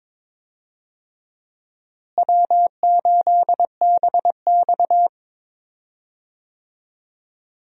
Morse code, heard as a keyed tone, about 22 wpm, 700 Hz. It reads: W8BX
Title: USA Call Signs - 4 Characters - Repeated 3x - 22wpm